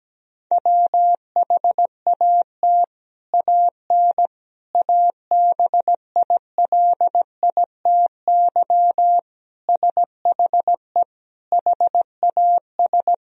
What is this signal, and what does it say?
Morse code, heard as a keyed tone, about 17 wpm, 700 Hz. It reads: WHAT AN ABILITY SHE HAS
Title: Sentences from Top 1000 Words - Review Part 3 - 17wpm